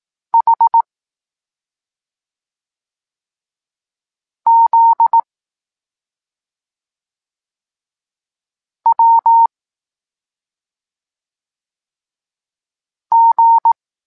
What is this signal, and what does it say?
Morse code, heard as a keyed tone, about 18 wpm, 920 Hz. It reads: H Z W G
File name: Tekst-23 4